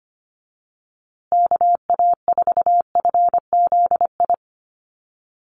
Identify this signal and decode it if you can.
Morse code, heard as a keyed tone, about 25 wpm, 700 Hz. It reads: KA4FZI
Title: USA Call Signs Repeated 2x 25wpm